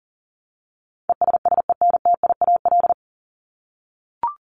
Morse code, heard as a keyed tone, about 40 wpm, 700 Hz. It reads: ESSENTIAL
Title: LY Suffix - Top 100 Words with Base Words 40wpm